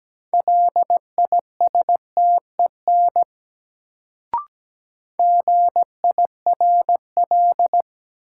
Morse code, heard as a keyed tone, about 17 wpm, 700 Hz. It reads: LISTEN GIRL
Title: Sentences from Top 400 Words - New Word Drill - 17wpm